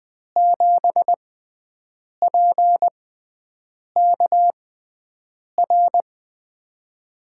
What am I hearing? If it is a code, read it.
Morse code, heard as a keyed tone, about 20 wpm, spaced as for 7 wpm, 700 Hz. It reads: 7PKR
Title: USA Call Signs - 6 Characters - 6x Character Spacing - 20wpm